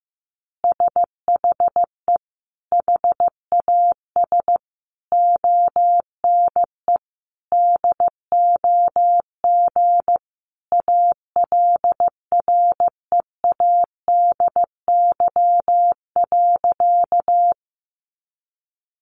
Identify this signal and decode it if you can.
Morse code, heard as a keyed tone, about 15 wpm, 700 Hz. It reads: SHE HAS ONE DOG ALREADY.
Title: Easy Phrases 15wpm